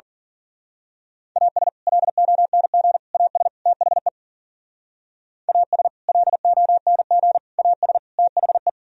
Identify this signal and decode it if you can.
Morse code, heard as a keyed tone, about 43 wpm, 700 Hz. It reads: AS LONG AS THE AS LONG AS THE
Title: Speed-Racing - Sets of 4 Words 40wpm